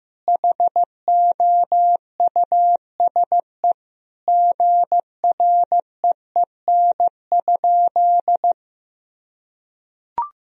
Morse code, heard as a keyed tone, about 15 wpm, 700 Hz. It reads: HOUSE GREEN?
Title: Sentences from Top 1000 Words - Review Part 2 - 15wpm